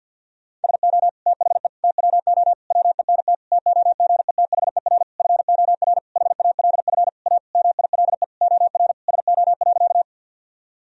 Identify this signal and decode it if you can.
Morse code, heard as a keyed tone, about 51 wpm, 700 Hz. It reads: SO THE TWO WENT TOGETHER FOR HALF A MILE OR SO.
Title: Sentences from Top 1000 Words - Review Part 1 - 50wpm